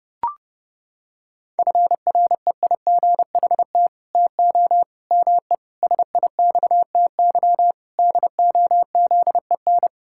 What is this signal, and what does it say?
Morse code, heard as a keyed tone, about 30 wpm, 700 Hz. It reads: FREIGHT TO ME SIXTY DOZEN
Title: Pangram Challenge v2 - 30wpm